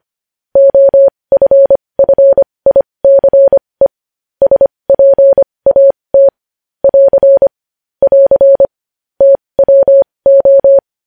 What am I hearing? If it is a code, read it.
Morse code, heard as a keyed tone, about 25 wpm, spaced as for 20 wpm, 550 Hz. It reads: OFFICE SPAT <AR> <AR> TWO
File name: podcast / News headlines-1x-25x20